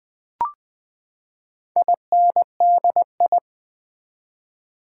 Morse code, heard as a keyed tone, about 20 wpm, 700 Hz. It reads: INDI
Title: Build A Word - 4 Letters At A Time - 10-Letter Words - 20wpm